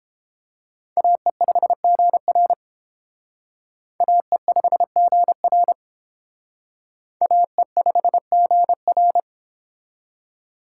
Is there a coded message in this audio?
Yes: Morse code from 1.0 to 9.2 s, heard as a keyed tone, about 28 wpm, 700 Hz. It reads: AE5GR AE5GR AE5GR